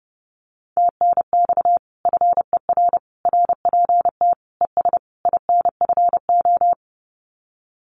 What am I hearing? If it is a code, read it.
Morse code, heard as a keyed tone, about 30 wpm, 700 Hz. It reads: TNX FER RPT ES INFO
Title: Common QSO Phrases Encore 30wpm